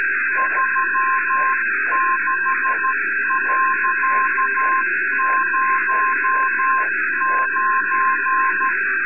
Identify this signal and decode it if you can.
Morse code, heard as a keyed tone, about 13 wpm, 1050 Hz, with noise. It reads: 36.09
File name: sat / RADIOSCAF-B 24Dec2011